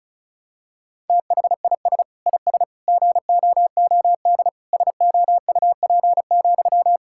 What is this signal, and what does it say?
Morse code, heard as a keyed tone, about 36 wpm, 690 Hz. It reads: THIS IS GOOD SOUP,
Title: Sentences from Top 1000 Words - Review Part 6 - 35wpm